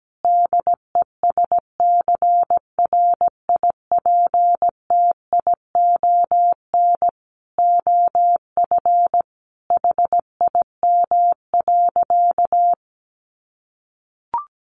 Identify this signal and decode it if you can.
Morse code, heard as a keyed tone, about 17 wpm, 700 Hz. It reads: DESCRIPTION OF HIM.
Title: Sentences from Top 1300 Words 17wpm